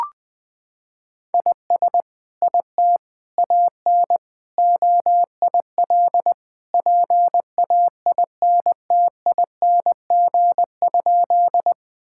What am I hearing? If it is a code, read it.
Morse code, heard as a keyed tone, about 20 wpm, 700 Hz. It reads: IS IT AN OIL PAINTING?
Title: Sentences from Top 1000 Words - Review Part 6 - 20wpm